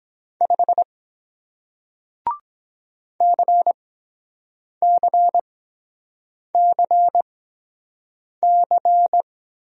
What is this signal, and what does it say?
Morse code, heard as a keyed tone, about 23 wpm, spaced as for 7 wpm, 700 Hz. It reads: 5 CCCC